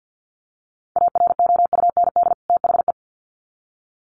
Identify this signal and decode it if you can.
Morse code, heard as a keyed tone, about 50 wpm, 700 Hz. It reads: AROUND THE
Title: Sets of 2 Words 50wpm